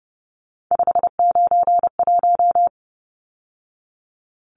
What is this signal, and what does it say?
Morse code, heard as a keyed tone, about 30 wpm, 700 Hz. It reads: 591
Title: Numbers - 3-Digit 30wpm